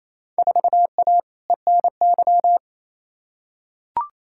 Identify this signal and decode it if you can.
Morse code, heard as a keyed tone, about 28 wpm, 700 Hz. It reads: 4A ENY
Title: Field Day Exchanges Encore - 28wpm